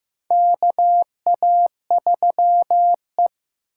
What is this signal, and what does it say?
Morse code, heard as a keyed tone, about 15 wpm, 700 Hz. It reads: KA3E
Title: USA Call Signs - 4 Characters - 2x Character Spacing - 15wpm